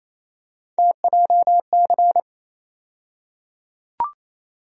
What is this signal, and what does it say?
Morse code, heard as a keyed tone, about 28 wpm, 700 Hz. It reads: TJC
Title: USA Call Sign Postfixes - 3 Letters 28wpm